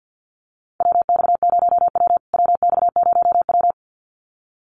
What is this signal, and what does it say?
Morse code, heard as a keyed tone, about 50 wpm, 700 Hz. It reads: WX0W WX0W
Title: USA Call Signs Repeated 2x 50wpm